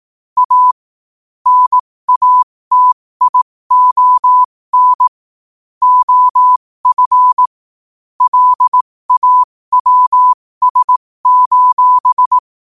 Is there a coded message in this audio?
Yes: Morse code from 0.4 to 12.4 s, heard as a keyed tone, about 18 wpm, spaced as for 15 wpm, 1000 Hz. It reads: A NATION OF LAWS: